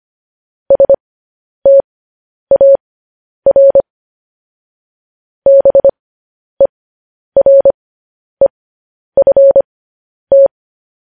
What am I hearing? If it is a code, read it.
Morse code, heard as a keyed tone, about 25 wpm, spaced as for 10 wpm, 550 Hz. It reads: STAR BEREFT